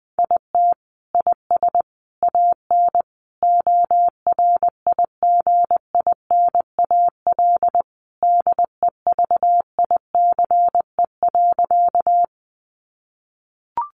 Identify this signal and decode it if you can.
Morse code, heard as a keyed tone, about 20 wpm, 700 Hz. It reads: IT IS AN ORIGINAL DEVICE.
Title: Sentences from Top 1500 Words 20wpm